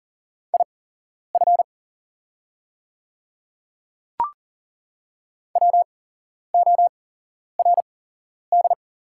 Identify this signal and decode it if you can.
Morse code, heard as a keyed tone, about 40 wpm, spaced as for 11 wpm, 690 Hz. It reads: IF WORD
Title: 8x Character Spacing - Top 200 Words - 40wpm